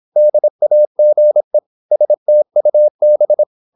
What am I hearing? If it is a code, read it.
Morse code, heard as a keyed tone, about 26 wpm, 600 Hz. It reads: DAGE STUB
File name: Korte ord DK 26wpm